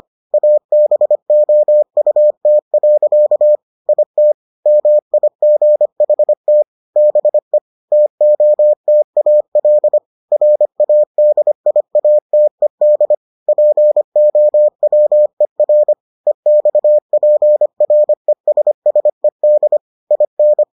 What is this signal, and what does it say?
Morse code, heard as a keyed tone, about 25 wpm, 600 Hz. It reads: ABOUT. IT MIGHT BE TOTAL RADIATED POWER EXPRESSED IN